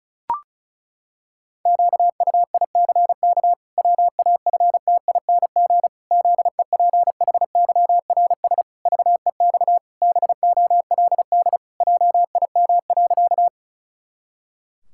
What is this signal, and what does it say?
Morse code, heard as a keyed tone, about 34 wpm, 700 Hz. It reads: QUICK WAFTING ZEPHYRS VEX BOLD JIM.
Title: Pangram Challenge v2 - 35wpm